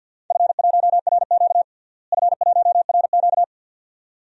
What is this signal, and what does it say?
Morse code, heard as a keyed tone, about 52 wpm, 700 Hz. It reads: F1RQ F1RQ
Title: DXCC-100 Call Signs - 4 Characters - Repeated 2x - 50wpm